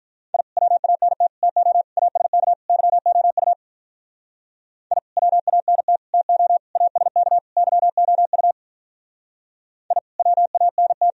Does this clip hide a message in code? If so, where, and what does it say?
Yes: Morse code from 0.3 to 11.1 s, heard as a keyed tone, about 45 wpm, 700 Hz. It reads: I WANT TO ASK YOU I WANT TO ASK YOU I WANT